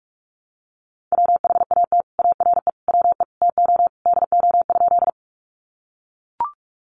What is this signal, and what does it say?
Morse code, heard as a keyed tone, about 44 wpm, 700 Hz. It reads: WHAT ARE WE TO DO?
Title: Sentences from Top 100 Words 45wpm